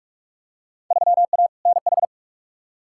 Morse code, heard as a keyed tone, about 45 wpm, 700 Hz. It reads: 3A NH
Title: Field Day Exchanges - 45wpm